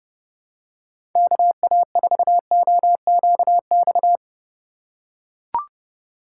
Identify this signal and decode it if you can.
Morse code, heard as a keyed tone, about 31 wpm, 700 Hz. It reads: KA4OQX
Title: USA Call Signs - 6 Characters - 2x Character Spacing - 30wpm